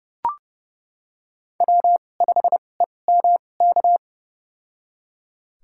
Morse code, heard as a keyed tone, about 30 wpm, spaced as for 22 wpm, 700 Hz. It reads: W5EMK
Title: USA Call Signs - 5 Characters - 2x Character Spacing - 30wpm